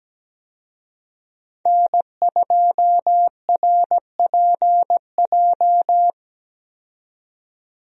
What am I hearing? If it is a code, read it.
Morse code, heard as a keyed tone, about 17 wpm, 700 Hz. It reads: N2RPJ
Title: USA Call Signs - 5 Characters - 17wpm